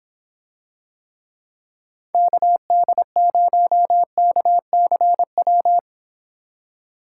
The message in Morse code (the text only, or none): KD0KCW